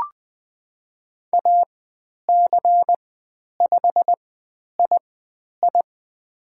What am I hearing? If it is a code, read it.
Morse code, heard as a keyed tone, about 20 wpm, spaced as for 10 wpm, 700 Hz. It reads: AC5II
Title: 102 Lesson 17g - Letter u - Call Signs 20wpm